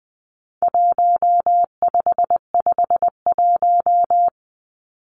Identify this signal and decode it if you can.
Morse code, heard as a keyed tone, about 20 wpm, 700 Hz. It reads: 1551
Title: Numbers - Contest Style - 4-Digit 20wpm